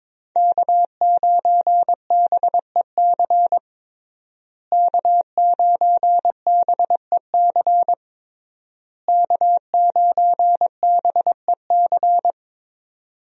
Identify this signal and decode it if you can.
Morse code, heard as a keyed tone, about 22 wpm, 700 Hz. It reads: K9BEC K9BEC K9BEC
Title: USA Call Signs - 5 Characters - Repeated 3x - 22wpm